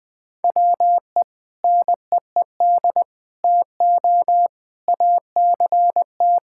Morse code, heard as a keyed tone, about 21 wpm, 700 Hz. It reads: WE NEED TO ACT
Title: Sentences with 2-Word Phrases - 20wpm